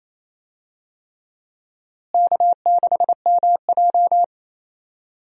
Morse code, heard as a keyed tone, about 27 wpm, 700 Hz. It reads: K6MJ
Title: USA Call Signs - 4 Characters - Repeated 3x - 28wpm